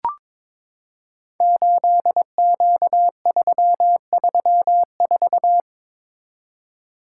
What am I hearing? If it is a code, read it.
Morse code, heard as a keyed tone, about 22 wpm, 700 Hz. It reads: 8Q334
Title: Code Groups - 5 Letter:Numbers - 22wpm